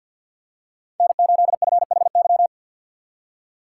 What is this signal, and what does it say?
Morse code, heard as a keyed tone, about 51 wpm, 700 Hz. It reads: N8FHY
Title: USA Call Signs - 5 Characters - 6x Character Spacing - 50wpm